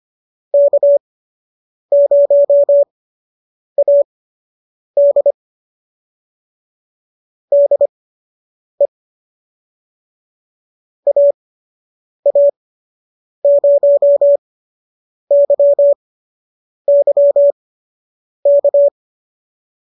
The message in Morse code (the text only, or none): K0AD DE AA0YY K